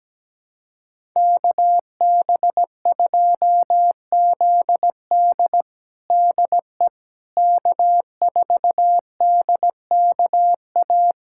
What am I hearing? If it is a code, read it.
Morse code, heard as a keyed tone, about 17 wpm, 700 Hz. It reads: KB2ZD DE K4DKA
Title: USA Callsign DE Callsign - 5 Characters - 17wpm